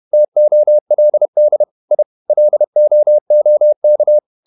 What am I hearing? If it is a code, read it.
Morse code, heard as a keyed tone, about 31 wpm, 600 Hz. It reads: TOLD I LOOK
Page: Never 31wpm | CW med Gnister